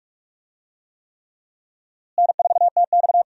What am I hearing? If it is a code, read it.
Morse code, heard as a keyed tone, about 46 wpm, 700 Hz. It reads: N4TX
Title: USA Call Signs - 4 Characters - Repeated 3x - 45wpm